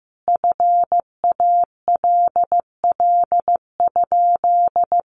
Morse code, heard as a keyed tone, about 15 wpm, 700 Hz. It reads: FALL?
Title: Sentences from Top 500 Words 15wpm